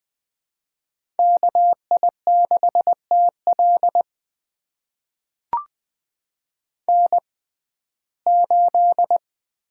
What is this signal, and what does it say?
Morse code, heard as a keyed tone, about 21 wpm, spaced as for 15 wpm, 700 Hz. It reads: KI6TL N8
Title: USA Call Signs - 5 Characters - 6x Character Spacing - 20wpm